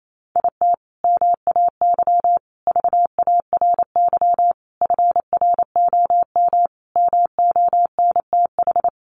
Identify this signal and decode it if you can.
Morse code, heard as a keyed tone, about 28 wpm, 700 Hz. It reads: IT MAY VARY FROM MONTH
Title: Sentences from Top 1200 Words 28wpm